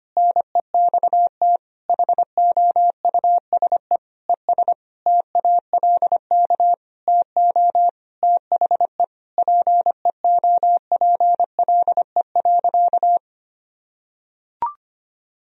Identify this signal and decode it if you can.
Morse code, heard as a keyed tone, about 25 wpm, 700 Hz. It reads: NEXT HOUSE ES TALK TO THE PEOPLE.
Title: Sentences from Top 400 Words and Abbreviations 25wpm